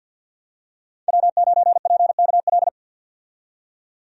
Morse code, heard as a keyed tone, about 51 wpm, 700 Hz. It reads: W9PKL
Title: USA Call Signs - 5 Characters - 50wpm